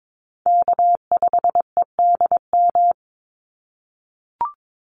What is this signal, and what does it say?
Morse code, heard as a keyed tone, about 22 wpm, 700 Hz. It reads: K5EDM
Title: Call Signs - 22wpm